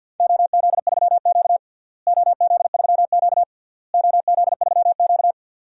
Morse code, heard as a keyed tone, about 52 wpm, 700 Hz. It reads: OZ3Q OZ3Q OZ3Q
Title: DXCC-100 Call Signs - 4 Characters - Repeated 3x - 50wpm